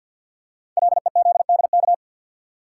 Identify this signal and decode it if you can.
Morse code, heard as a keyed tone, about 50 wpm, 690 Hz. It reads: LEZDK
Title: Code Groups - 5 Letters - 50wpm